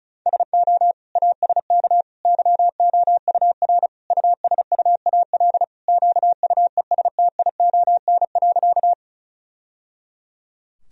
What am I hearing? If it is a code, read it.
Morse code, heard as a keyed tone, about 34 wpm, 700 Hz. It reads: SO ASK YOUR USUAL QUESTION.
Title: Sentences from Top 1000 Words - Review Part 2 - 35wpm